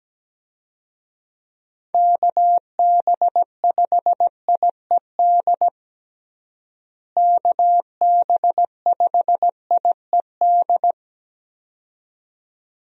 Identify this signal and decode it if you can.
Morse code, heard as a keyed tone, about 17 wpm, 700 Hz. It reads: KB5IED KB5IED
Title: USA Call Signs - 6 Characters - Repeated 2x - 17wpm